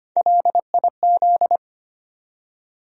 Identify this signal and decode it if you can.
Morse code, heard as a keyed tone, about 25 wpm, 700 Hz. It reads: LIZ
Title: USA Common Names - Max 3 Letters - 25wpm